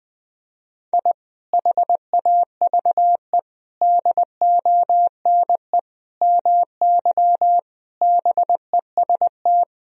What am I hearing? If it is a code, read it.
Morse code, heard as a keyed tone, about 20 wpm, 700 Hz. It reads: I HAVE DONE MY BEST
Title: Sentences from Top 1000 Words - Review Part 4 - 20wpm